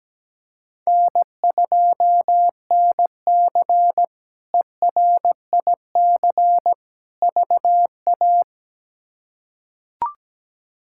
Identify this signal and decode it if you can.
Morse code, heard as a keyed tone, about 17 wpm, 700 Hz. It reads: N2NC ERIC VA
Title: K1USN SST QSO Practice - 17wpm